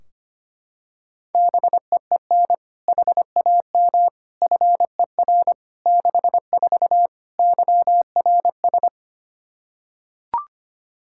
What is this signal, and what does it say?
Morse code, heard as a keyed tone, about 25 wpm, 700 Hz. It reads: BEEN HAM FER 64 YRS
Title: K1USN SST QSO Practice 25wpm